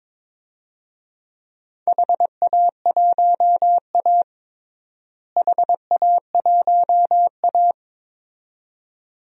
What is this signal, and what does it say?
Morse code, heard as a keyed tone, about 21 wpm, 700 Hz. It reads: HA1A HA1A
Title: DXCC-100 Call Signs - 4 Characters - Repeated 2x - 22wpm